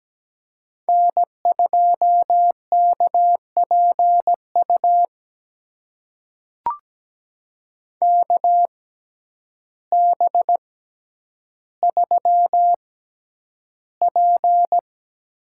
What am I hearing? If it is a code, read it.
Morse code, heard as a keyed tone, about 17 wpm, spaced as for 9 wpm, 700 Hz. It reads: N2KPU KB3P
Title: USA Call Signs - 6x Character Spacing - 17wpm